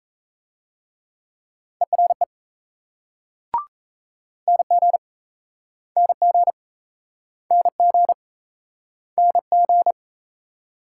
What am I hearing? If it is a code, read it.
Morse code, heard as a keyed tone, about 35 wpm, 700 Hz. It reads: ERE NG NG NG NG
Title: Speed-Racing Multi-letter Phonemes - 28wpm